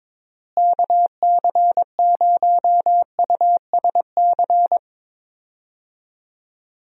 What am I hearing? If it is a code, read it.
Morse code, heard as a keyed tone, about 22 wpm, 700 Hz. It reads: KC0USC
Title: USA Call Signs - 6 Characters - Repeated 3x - 22wpm